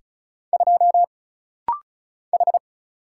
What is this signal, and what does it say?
Morse code, heard as a keyed tone, about 34 wpm, 700 Hz. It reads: 2 H
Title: Single Letter-Number Alt - Rapid-Fire 35wpm